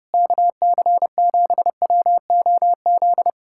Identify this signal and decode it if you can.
Morse code, heard as a keyed tone, about 31 wpm, 700 Hz. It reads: KC7WOZ